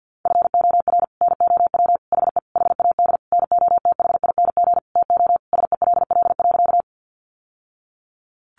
Morse code, heard as a keyed tone, about 50 wpm, 700 Hz. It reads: FOR NOW HE HAD NOTHING TO SELL.
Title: Sentences from Top 1100 Words 50wpm